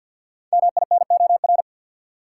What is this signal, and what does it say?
Morse code, heard as a keyed tone, about 50 wpm, 700 Hz. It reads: MINOR
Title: Five Letter Words - 50wpm